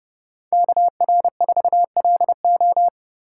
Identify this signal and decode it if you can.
Morse code, heard as a keyed tone, about 39 wpm, 700 Hz. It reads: KR4LO